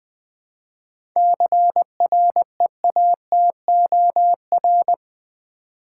Morse code, heard as a keyed tone, about 20 wpm, 690 Hz. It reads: CREATOR
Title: 7 Letter Words - First 3 Letters Given - 20wpm